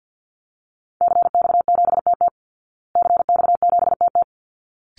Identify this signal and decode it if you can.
Morse code, heard as a keyed tone, about 50 wpm, 700 Hz. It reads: CX7TT CX7TT
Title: Call Signs Repeated 3x - 50wpm